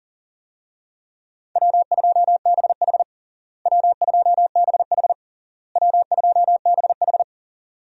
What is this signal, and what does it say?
Morse code, heard as a keyed tone, about 40 wpm, 700 Hz. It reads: W2BH W2BH W2BH
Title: USA Call Signs - 4 Characters - Repeated 3x - 40wpm